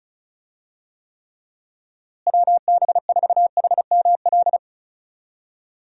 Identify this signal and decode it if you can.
Morse code, heard as a keyed tone, about 34 wpm, 700 Hz. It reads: WB4HML
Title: USA Call Signs - 6 Characters - Repeated 2x - 35wpm